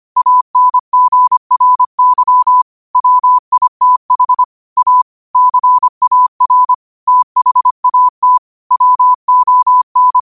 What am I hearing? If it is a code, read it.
Morse code, heard as a keyed tone, about 25 wpm, 1000 Hz. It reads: ANGRY WITH A CAR THAT WON
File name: qotdfiles / 25wpm